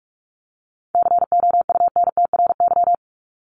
Morse code, heard as a keyed tone, about 44 wpm, 700 Hz. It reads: COUNTRY
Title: ES:S Suffix - Top 100 Words with Base Words 45wpm